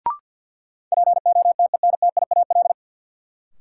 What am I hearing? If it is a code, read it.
Morse code, heard as a keyed tone, about 53 wpm, 700 Hz. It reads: POTENTIAL
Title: Sentences from Top 1000 Words - Review Part 4 - New Word Drill - 50wpm